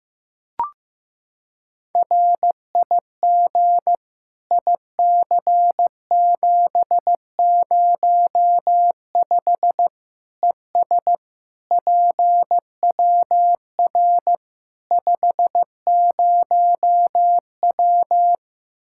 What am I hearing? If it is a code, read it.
Morse code, heard as a keyed tone, about 15 wpm, 700 Hz. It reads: RIG IC705 ES PWR 50W
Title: K1USN SST QSO Practice 15wpm